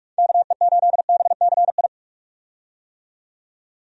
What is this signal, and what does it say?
Morse code, heard as a keyed tone, about 46 wpm, 700 Hz. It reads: KE8BCI
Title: USA Call Signs - 6 Characters - 45wpm